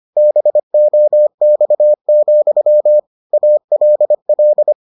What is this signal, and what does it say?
Morse code, heard as a keyed tone, about 25 wpm, 600 Hz. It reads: BOX, ALL